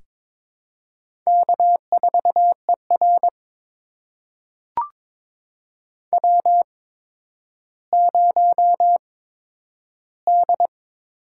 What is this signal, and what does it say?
Morse code, heard as a keyed tone, about 22 wpm, spaced as for 11 wpm, 700 Hz. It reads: K4ER W0D
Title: USA Call Signs - 8x Character Spacing - 22wpm